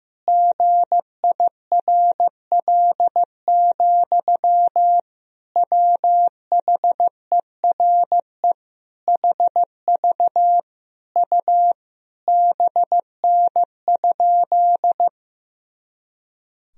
Morse code, heard as a keyed tone, about 15 wpm, 700 Hz. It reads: GIRL, WHERE HV U BN?
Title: Sentences from Top 400 Words and Abbreviations 15wpm